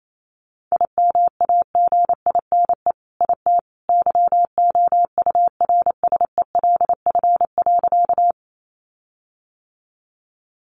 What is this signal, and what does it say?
Morse code, heard as a keyed tone, about 28 wpm, 700 Hz. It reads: IMAGINE IT YOURSELF.
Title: Sentences from Top 1400 Words 28wpm